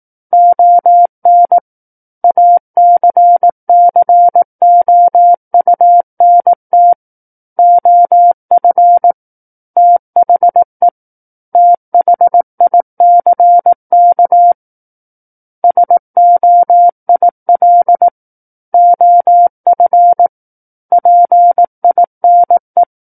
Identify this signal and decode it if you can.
Morse code, heard as a keyed tone, about 18 wpm, 700 Hz. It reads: ON ACCOUNT OF THE THICK SOIL OF PINE
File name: Book One / 09-Chapter 9 - 18 WPM